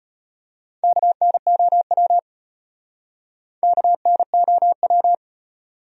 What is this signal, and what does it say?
Morse code, heard as a keyed tone, about 34 wpm, 700 Hz. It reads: KNOW KNOW